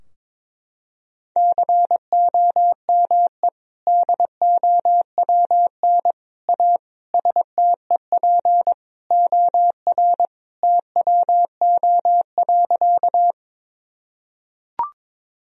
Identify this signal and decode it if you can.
Morse code, heard as a keyed tone, about 21 wpm, 700 Hz. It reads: COME DOWN A STEP OR TWO.
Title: Sentences from Top 1000 Words - Review Part 2 - 22wpm